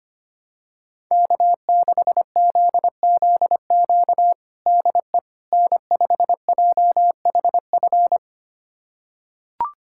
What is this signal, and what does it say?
Morse code, heard as a keyed tone, about 25 wpm, 700 Hz. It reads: K6ZZQ DE N5JHF
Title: USA Callsign DE Callsign - 5 Characters - 25wpm